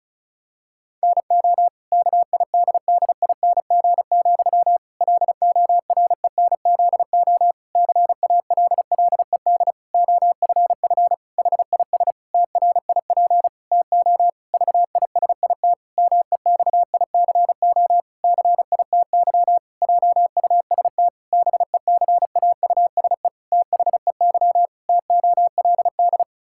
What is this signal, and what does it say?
Morse code, heard as a keyed tone, about 34 wpm, 700 Hz. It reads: NO KIDDING, LORENZO CALLED OFF HIS TRIP TO VISIT MEXICO CITY JUST BECAUSE THEY TOLD
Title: Pangram Challenge v2 - 35wpm